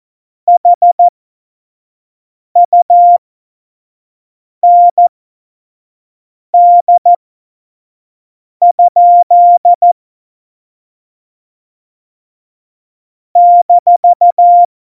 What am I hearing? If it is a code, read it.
Morse code, heard as a keyed tone, about 14 wpm, spaced as for 5 wpm, 700 Hz. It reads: HUND? -
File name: Morsekurs / Witze / Witz 03. Oktober 17, Tempo 25 F70